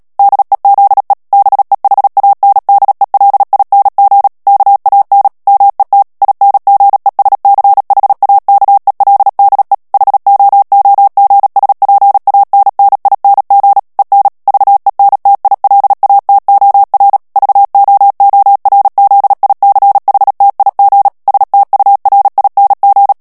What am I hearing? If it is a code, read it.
Morse code, heard as a keyed tone, about 37 wpm, 800 Hz. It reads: DEZE BEHANDELING KAN MET INGESCHAKELDE HOOGSPANNING EN VENTILATOR VOORZICHTIG STURING
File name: morse mp3 / HOLLANDAIS texte clair 2 vitesse 35 mots minute